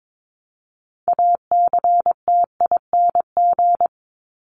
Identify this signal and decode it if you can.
Morse code, heard as a keyed tone, about 22 wpm, 700 Hz. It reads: ACTING
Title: ING Suffix - Top 100 Words with Base Words 22wpm